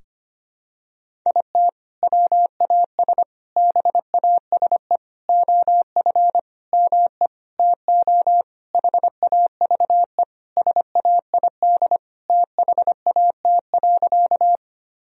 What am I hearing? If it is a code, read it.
Morse code, heard as a keyed tone, about 25 wpm, 700 Hz. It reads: IT WAS BASE OF ME TO HAVE SAID THAT.
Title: Sentences from Top 1000 Words - Review Part 1 - 25wpm